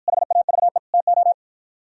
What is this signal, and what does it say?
Morse code, heard as a keyed tone, about 53 wpm, 700 Hz. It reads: HAVE TO